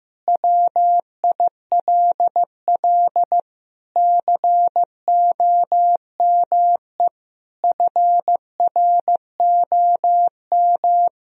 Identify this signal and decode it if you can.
Morse code, heard as a keyed tone, about 15 wpm, 700 Hz. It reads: WILL COME FROM
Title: Sentences from Top 1000 Words - Review Part 4 - 15wpm